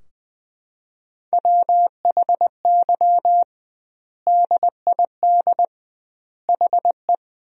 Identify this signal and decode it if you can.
Morse code, heard as a keyed tone, about 20 wpm, 700 Hz. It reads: WHY DID HE
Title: 2x Word-Spacing - Sentences from Top 200 Words - 20wpm